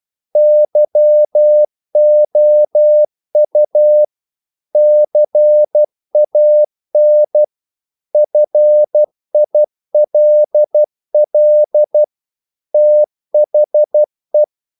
Morse code, heard as a keyed tone, about 12 wpm, 600 Hz. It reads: YOU CAN FILL THE
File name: CW / Practice / If12